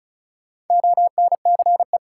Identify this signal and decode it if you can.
Morse code, heard as a keyed tone, about 35 wpm, 700 Hz. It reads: ONCE
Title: Sentences from Top 300 Words - New Word Drill - 35wpm